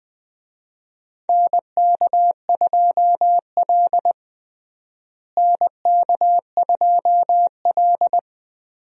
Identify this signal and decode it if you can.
Morse code, heard as a keyed tone, about 20 wpm, 700 Hz. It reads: NK2L NK2L
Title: USA Call Signs - 4 Characters - Repeated 3x - 20wpm